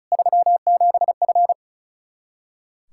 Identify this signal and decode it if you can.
Morse code, heard as a keyed tone, about 34 wpm, 700 Hz. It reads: 37F
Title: K1USN SST QSO Practice 35wpm